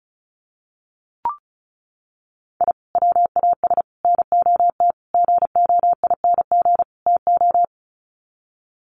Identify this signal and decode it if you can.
Morse code, heard as a keyed tone, about 34 wpm, 700 Hz. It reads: I WAS NOT GOING TO
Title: Sets of 5 Words 35wpm